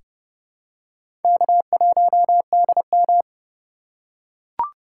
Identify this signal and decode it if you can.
Morse code, heard as a keyed tone, about 30 wpm, 700 Hz. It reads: K1DM
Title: K1USN SST QSO Practice - 30wpm